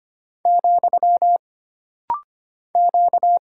Morse code, heard as a keyed tone, about 25 wpm, 700 Hz. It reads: , Q
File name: Single Character (Letter:Number:Punctuation:Prosign) - Rapid-Fire - 25wpm